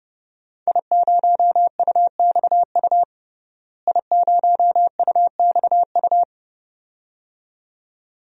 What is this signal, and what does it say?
Morse code, heard as a keyed tone, about 32 wpm, 700 Hz. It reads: I0UXU I0UXU
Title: DXCC-100 Call Signs - 5 Characters - Repeated 2x - 30wpm